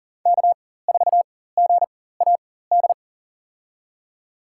Morse code, heard as a keyed tone, about 40 wpm, spaced as for 19 wpm, 700 Hz. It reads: K4GAD
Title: USA Call Signs - 5 Characters - 4x Character Spacing - 40wpm